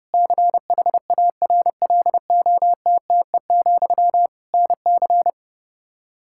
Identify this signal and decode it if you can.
Morse code, heard as a keyed tone, about 32 wpm, 700 Hz. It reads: CHARLOTTE, NC
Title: Practice QSO - Exchange 1 - Beginning - 30wpm